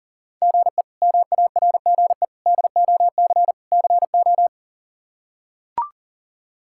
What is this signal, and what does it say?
Morse code, heard as a keyed tone, about 37 wpm, 700 Hz. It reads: GE MARGE DOC CO
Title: K1USN SST QSO Practice - 40wpm